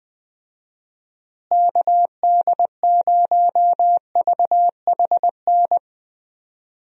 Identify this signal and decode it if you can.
Morse code, heard as a keyed tone, about 21 wpm, 700 Hz. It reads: KD0VHN
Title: USA Call Signs - 6 Characters - Repeated 2x - 20wpm